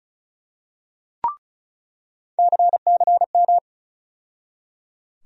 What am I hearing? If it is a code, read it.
Morse code, heard as a keyed tone, about 36 wpm, 700 Hz. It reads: CCM
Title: USA Call Sign Postfixes - 3 Letters 35wpm